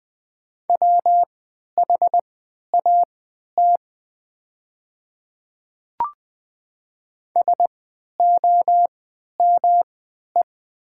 Morse code, heard as a keyed tone, about 20 wpm, spaced as for 11 wpm, 700 Hz. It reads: WHAT SOME
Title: 3x Character Spacing - Top 100 Words - 20wpm